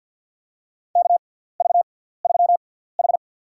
Morse code, heard as a keyed tone, about 53 wpm, spaced as for 17 wpm, 700 Hz. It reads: KV3H
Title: USA Call Signs - 4 Characters - 6x Character Spacing - 50wpm